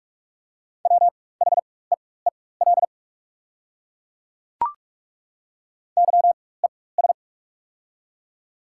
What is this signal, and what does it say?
Morse code, heard as a keyed tone, about 42 wpm, spaced as for 21 wpm, 690 Hz. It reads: WHEEL YES
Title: 4x Character Spacing - Top 500 Words - 45wpm